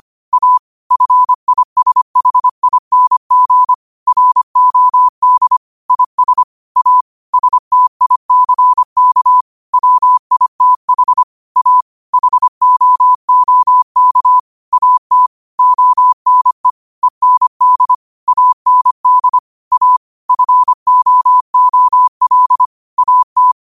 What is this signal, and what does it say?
Morse code, heard as a keyed tone, about 25 wpm, 1000 Hz. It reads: A FISHING ROD IS A STICK WITH A HOOK AT ONE END AND A FOOL AT